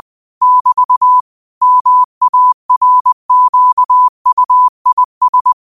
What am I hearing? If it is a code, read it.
Morse code, heard as a keyed tone, about 20 wpm, 1000 Hz. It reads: <BT> MARQUIS
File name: qotdfiles / 20wpm